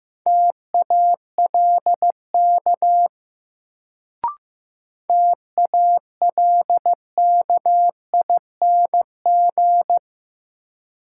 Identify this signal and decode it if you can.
Morse code, heard as a keyed tone, about 15 wpm, 700 Hz. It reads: TALK TALKING
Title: Sets of 2 Words - New Word Drill - 15wpm